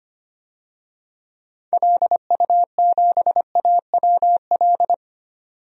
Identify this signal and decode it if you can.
Morse code, heard as a keyed tone, about 25 wpm, 700 Hz. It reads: LU7AWL
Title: DXCC-100 Call Signs - 6 Characters - 25wpm